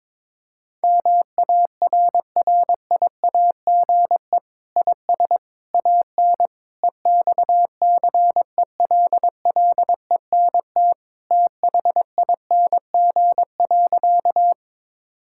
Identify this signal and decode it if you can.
Morse code, heard as a keyed tone, about 22 wpm, 700 Hz. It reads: MARRIAGE IS AN EXCELLENT THING.
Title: Sentences from Top 1000 Words - Review Part 5 - 22wpm